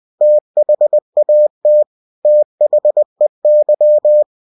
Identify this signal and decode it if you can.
Morse code, heard as a keyed tone, about 20 wpm, 600 Hz. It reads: THAT THEY
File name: cw-books / winnie-the-pooh / winnie-20wpm-page-0074